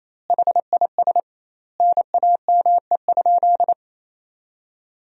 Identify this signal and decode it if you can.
Morse code, heard as a keyed tone, about 28 wpm, 700 Hz. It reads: HIS NAME?
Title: 2x Word-Spacing - Sentences from Top 200 Words - 28wpm